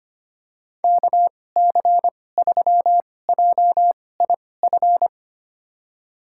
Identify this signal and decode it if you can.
Morse code, heard as a keyed tone, about 25 wpm, spaced as for 18 wpm, 700 Hz. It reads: KC3JIF
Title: USA Call Signs - 6 Characters - 2x Character Spacing - 25wpm